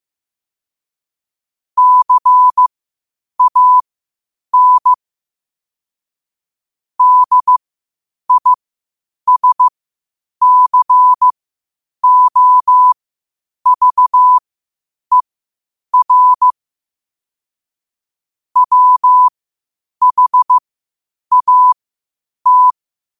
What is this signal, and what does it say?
Morse code, heard as a keyed tone, about 15 wpm, spaced as for 8 wpm, 1000 Hz. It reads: CAN DISCOVER WHAT
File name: qotdfiles / 8wpm